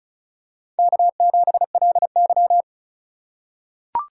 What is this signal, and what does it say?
Morse code, heard as a keyed tone, about 36 wpm, 700 Hz. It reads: K7LY
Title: USA Call Signs - 4 Characters - 3x Character Spacing - 35wpm